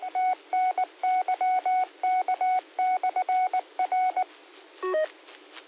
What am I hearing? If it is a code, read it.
Morse code, heard as a keyed tone, about 19 wpm, 740 Hz. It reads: TNYK/R